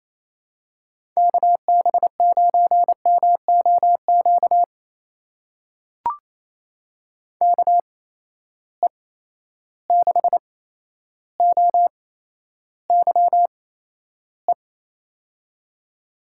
Morse code, heard as a keyed tone, about 28 wpm, spaced as for 12 wpm, 700 Hz. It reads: KB9MOQ KE6OYE
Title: USA Call Signs - 6 Characters - 8x Character Spacing - 28wpm